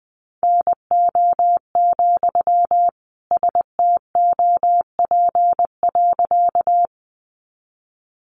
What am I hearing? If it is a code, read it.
Morse code, heard as a keyed tone, about 20 wpm, 700 Hz. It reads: NO, STOP.
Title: Sentences from Top 300 Words 20wpm